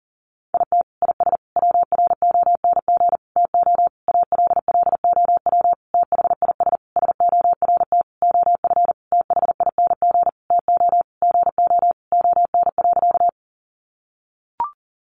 Morse code, heard as a keyed tone, about 40 wpm, 700 Hz. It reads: IT IS WRONG TO ALLOW THIS SORT OF THING TO GO ON.
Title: Sentences from Top 1300 Words 40wpm